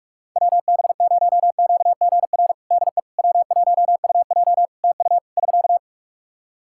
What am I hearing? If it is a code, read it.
Morse code, heard as a keyed tone, about 43 wpm, 700 Hz. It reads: WB0QGR DE W1UJ TU <SK>
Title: Practice QSO - Exchange 4 - Ending - 45wpm